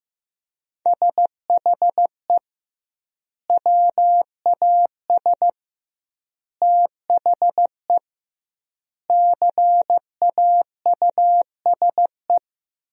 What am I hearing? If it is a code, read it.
Morse code, heard as a keyed tone, about 15 wpm, 700 Hz. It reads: SHE WAS THE CAUSE
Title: 2x Word-Spacing - Sentences from Top 200 Words - 15wpm